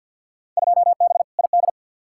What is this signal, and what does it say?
Morse code, heard as a keyed tone, about 50 wpm, 700 Hz. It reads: 2B ID
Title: Field Day Exchanges Encore - 50wpm